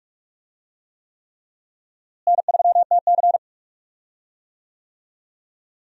Morse code, heard as a keyed tone, about 42 wpm, 700 Hz. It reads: N3TC
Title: USA Call Signs - 4 Characters - Repeated 3x - 45wpm